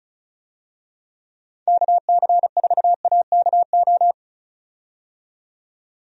Morse code, heard as a keyed tone, about 36 wpm, 700 Hz. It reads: KC4AKO
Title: USA Call Signs - 6 Characters - 35wpm